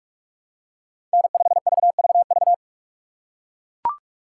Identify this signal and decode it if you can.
Morse code, heard as a keyed tone, about 43 wpm, 700 Hz. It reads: N5VVV
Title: USA Call Signs - 5 Characters - 3x Character Spacing - 45wpm